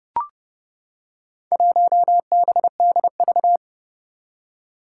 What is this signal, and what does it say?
Morse code, heard as a keyed tone, about 31 wpm, 700 Hz. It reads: 1BDV
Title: Four Challenging Letter-Number Combinations 30wpm